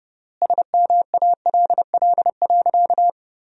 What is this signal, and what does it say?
Morse code, heard as a keyed tone, about 30 wpm, 700 Hz. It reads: SMALL.
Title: 3x Word-Spacing - Sentences from Top 400 Words - 30wpm